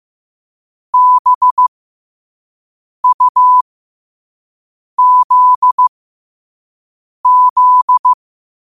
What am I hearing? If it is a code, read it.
Morse code, heard as a keyed tone, about 15 wpm, spaced as for 5 wpm, 1000 Hz. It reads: BUZZ